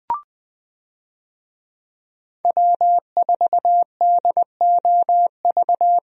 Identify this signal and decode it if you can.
Morse code, heard as a keyed tone, about 20 wpm, 700 Hz. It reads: W4DOV
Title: USA Call Signs - 5 Characters - Repeated 3x - 20wpm